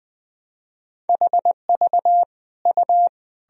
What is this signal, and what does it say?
Morse code, heard as a keyed tone, about 21 wpm, 700 Hz. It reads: HV U
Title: Sentences from Top 300 Words and Abbreviations 20wpm